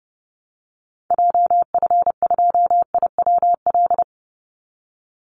Morse code, heard as a keyed tone, about 30 wpm, 700 Hz. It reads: JF2IWL
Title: Call Signs Repeated 3x - 30wpm